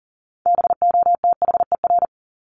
Morse code, heard as a keyed tone, about 40 wpm, 700 Hz. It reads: BOTHER
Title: Top 1501-1750 Words 40wpm